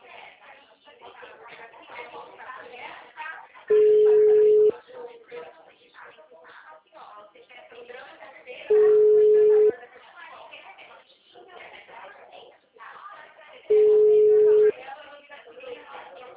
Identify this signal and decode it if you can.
Ringback tone (Europe, 425 Hz), with speech.